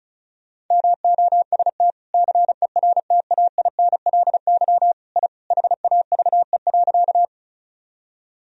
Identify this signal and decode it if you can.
Morse code, heard as a keyed tone, about 36 wpm, 700 Hz. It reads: MOST CERTAINLY I HAVE.
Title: Sentences from Top 1000 Words - Review Part 1 - 35wpm